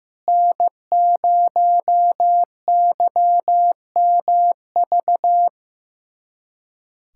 Morse code, heard as a keyed tone, about 15 wpm, 700 Hz. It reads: N0YMV
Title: USA Callsign DE Callsign - 5 Characters - 15wpm